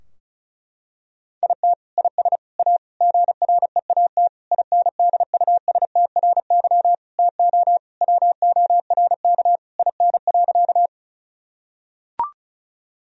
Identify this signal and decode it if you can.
Morse code, heard as a keyed tone, about 34 wpm, 700 Hz. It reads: IT IS A GREAT INDUSTRY TO WORK IN.
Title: Sentences from Top 1000 Words - Review Part 4 - 35wpm